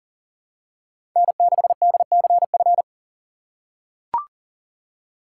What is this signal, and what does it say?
Morse code, heard as a keyed tone, about 39 wpm, 700 Hz. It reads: N6DCF
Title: USA Call Signs - 5 Characters - 3x Character Spacing - 40wpm